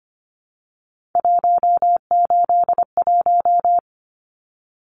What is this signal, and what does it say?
Morse code, heard as a keyed tone, about 25 wpm, 700 Hz. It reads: 181
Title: Numbers - 3-Digit 25wpm